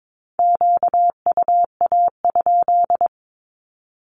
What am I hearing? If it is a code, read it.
Morse code, heard as a keyed tone, about 22 wpm, 700 Hz. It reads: QUA?
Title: Q-Codes with Spoken Meaning - 22wpm